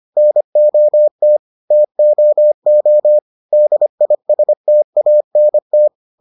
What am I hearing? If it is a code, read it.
Morse code, heard as a keyed tone, about 25 wpm, 600 Hz. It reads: NOT TOO DISTANT